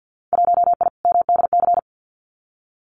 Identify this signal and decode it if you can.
Morse code, heard as a keyed tone, about 50 wpm, 700 Hz. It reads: 1I MDC
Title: Winter Field Day Exchanges 50wpm